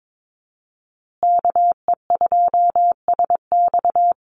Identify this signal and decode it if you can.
Morse code, heard as a keyed tone, about 22 wpm, 690 Hz. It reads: KE2SX
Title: Call Signs Repeated 2x - 22wpm